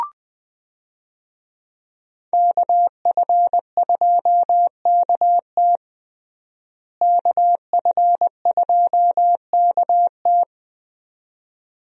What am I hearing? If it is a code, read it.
Morse code, heard as a keyed tone, about 20 wpm, 700 Hz. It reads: KF2KT KF2KT
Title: USA Call Signs - 5 Characters - Repeated 2x - 20wpm